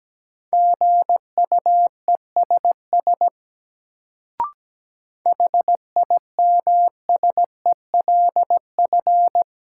Morse code, heard as a keyed tone, about 17 wpm, 700 Hz. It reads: GUESS HIMSELF
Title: Sentences from Top 1500 Words - New Word Drill - 17wpm